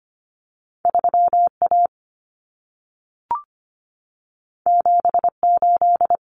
Morse code, heard as a keyed tone, about 25 wpm, 700 Hz. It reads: 3A 78
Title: Common Cut Numbers - 2-Digit 25wpm